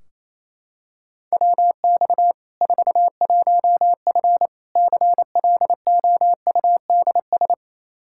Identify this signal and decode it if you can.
Morse code, heard as a keyed tone, about 28 wpm, 700 Hz. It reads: WX 41F CLOUDS
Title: Practice QSO - Exchange 2 - Info - 28wpm